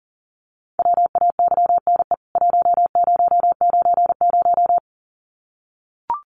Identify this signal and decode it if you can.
Morse code, heard as a keyed tone, about 40 wpm, 700 Hz. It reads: WAYNE 1090
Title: CWT Exchange 40wpm